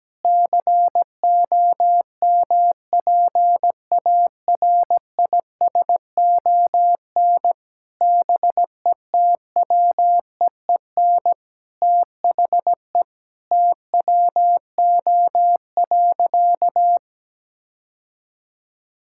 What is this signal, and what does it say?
Morse code, heard as a keyed tone, about 17 wpm, 700 Hz. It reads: COMPARISON BETWEEN THE TWO.
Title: Sentences from Top 1600 Words 17wpm